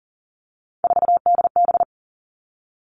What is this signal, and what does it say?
Morse code, heard as a keyed tone, about 40 wpm, 700 Hz. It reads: 4DB
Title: All Cut Numbers - 3-Digit 40wpm